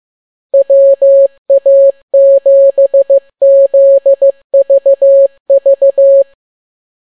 Morse code, heard as a keyed tone, about 15 wpm, 550 Hz. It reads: WA7ZVV